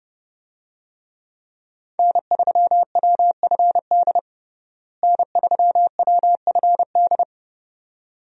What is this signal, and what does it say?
Morse code, heard as a keyed tone, about 30 wpm, 700 Hz. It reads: N3WFD N3WFD
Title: USA Call Signs - 5 Characters - Repeated 2x - 30wpm